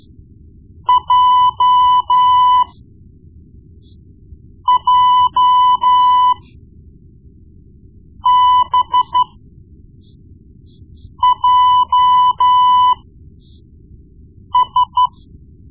Morse code, heard as a keyed tone, about 13 wpm, spaced as for 4 wpm, 1000 Hz. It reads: JJBJS
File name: Morse Lessons / Morse Lessons / Tape-1B